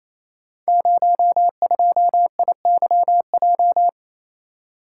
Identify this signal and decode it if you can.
Morse code, heard as a keyed tone, about 28 wpm, 700 Hz. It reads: 02IYJ
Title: Code Groups - 5 Letter:Numbers - 28wpm